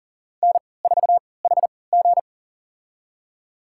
Morse code, heard as a keyed tone, about 40 wpm, spaced as for 23 wpm, 700 Hz. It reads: N4HG
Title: USA Call Signs - 4 Characters - 3x Character Spacing - 40wpm